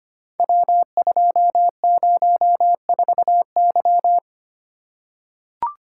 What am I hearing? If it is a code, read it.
Morse code, heard as a keyed tone, about 25 wpm, 700 Hz. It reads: W204Y
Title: Code Groups - 5 Letter:Numbers - 25wpm